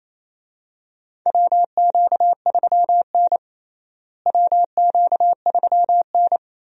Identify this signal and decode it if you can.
Morse code, heard as a keyed tone, about 28 wpm, 700 Hz. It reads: WQ3N WQ3N
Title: USA Call Signs - 4 Characters - Repeated 3x - 28wpm